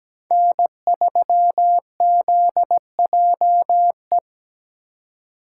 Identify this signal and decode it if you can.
Morse code, heard as a keyed tone, about 17 wpm, 700 Hz. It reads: N3ZJE
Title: USA Call Signs - 5 Characters - 8x Character Spacing - 17wpm